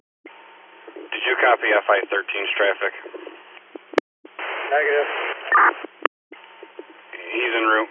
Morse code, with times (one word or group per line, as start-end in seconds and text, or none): none